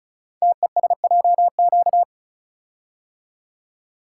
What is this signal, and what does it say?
Morse code, heard as a keyed tone, about 36 wpm, 700 Hz. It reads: TESJQ
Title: Code Groups - 5 Letters - 35wpm